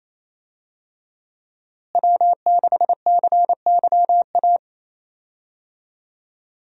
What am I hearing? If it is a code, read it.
Morse code, heard as a keyed tone, about 28 wpm, 700 Hz. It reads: W6CYA
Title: USA Call Signs - 5 Characters - Repeated 2x - 28wpm